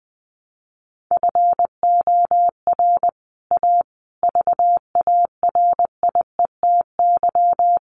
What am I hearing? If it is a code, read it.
Morse code, heard as a keyed tone, about 20 wpm, 700 Hz. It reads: FOR A VARIETY
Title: Sets of 5 Words Encore 20wpm